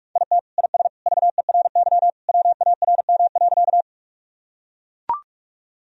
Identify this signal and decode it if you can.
Morse code, heard as a keyed tone, about 41 wpm, 700 Hz. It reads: IT IS VERY WARM.
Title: Sentences from Top 1000 Words - Review Part 2 - 45wpm